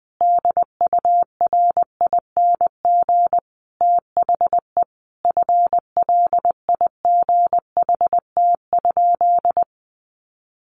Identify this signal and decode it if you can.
Morse code, heard as a keyed tone, about 20 wpm, 700 Hz. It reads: DURING THE FLIGHT?
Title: Sentences from Top 1000 Words 20wpm